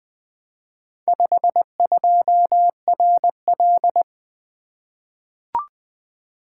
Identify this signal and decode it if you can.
Morse code, heard as a keyed tone, about 20 wpm, 700 Hz. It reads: 52RL
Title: Four Letter-Number Combinations - 20wpm